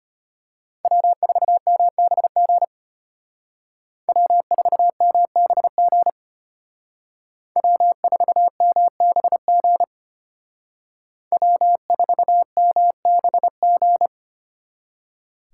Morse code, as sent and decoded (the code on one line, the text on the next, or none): .-- ....- -- -... --. / .-- ....- -- -... --. / .-- ....- -- -... --. / .-- ....- -- -... --.
W4MBG W4MBG W4MBG W4MBG